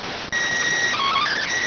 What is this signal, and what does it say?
SSTV (slow-scan television), PD 290.